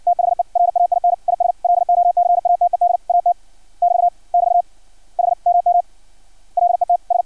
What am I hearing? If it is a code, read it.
Morse code, heard as a keyed tone, about 56 wpm, 700 Hz. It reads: THE RAIN IS LOCATED AT <BT> <BT> HMM LETS
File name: archived / CRSnet